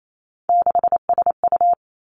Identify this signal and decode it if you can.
Morse code, heard as a keyed tone, about 28 wpm, 700 Hz. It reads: 6SU
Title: Three Challenging Letter-Number Combinations 28wpm